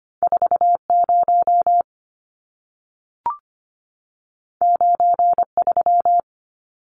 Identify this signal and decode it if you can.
Morse code, heard as a keyed tone, about 25 wpm, 700 Hz. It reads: 40 93
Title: Numbers - 2-Digit 25wpm